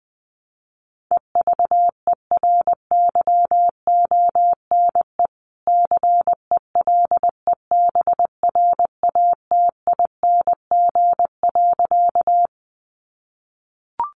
Morse code, heard as a keyed tone, about 20 wpm, 700 Hz. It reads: EVERYONE CELEBRATING.
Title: A Sound Of Thunder 20wpm